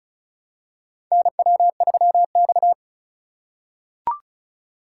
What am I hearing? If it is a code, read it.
Morse code, heard as a keyed tone, about 36 wpm, 700 Hz. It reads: NW3X
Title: USA Call Signs - 4 Characters - 2x Character Spacing - 35wpm